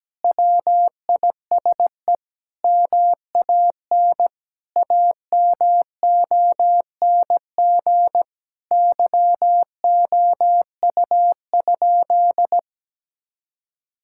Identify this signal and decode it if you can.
Morse code, heard as a keyed tone, about 17 wpm, 700 Hz. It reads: WISE MAN AMONG YOU?
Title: Sentences from Top 1600 Words 17wpm